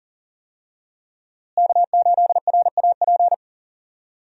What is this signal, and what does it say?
Morse code, heard as a keyed tone, about 40 wpm, 700 Hz. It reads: K8RAP
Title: USA Call Signs - 5 Characters - Repeated 3x - 40wpm